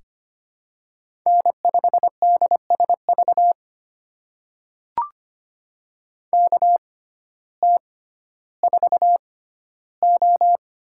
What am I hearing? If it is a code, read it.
Morse code, heard as a keyed tone, about 25 wpm, spaced as for 14 wpm, 700 Hz. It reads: N5DSV KT4O
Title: USA Call Signs - 5 Characters - 6x Character Spacing - 25wpm